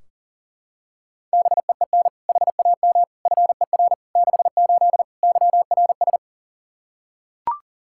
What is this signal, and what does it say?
Morse code, heard as a keyed tone, about 40 wpm, 700 Hz. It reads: BEEN HAM FER 68 YRS
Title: Practice QSO - Exchange 3 - More Info - 40wpm